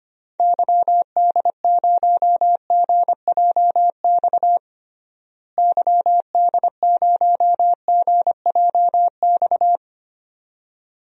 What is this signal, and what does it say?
Morse code, heard as a keyed tone, about 25 wpm, 700 Hz. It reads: YD0GJX YD0GJX
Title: DXCC-100 Call Signs - 6 Characters - Repeated 2x - 25wpm